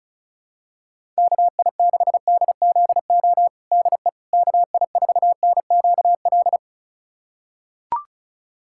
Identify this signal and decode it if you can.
Morse code, heard as a keyed tone, about 34 wpm, 700 Hz. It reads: KI6DZO DE KI4NQL
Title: USA Callsign DE Callsign 35wpm